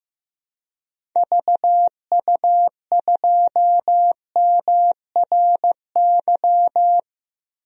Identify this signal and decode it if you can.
Morse code, heard as a keyed tone, about 15 wpm, 700 Hz. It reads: VU2MRY
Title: DXCC-100 Call Signs - 6 Characters - 15wpm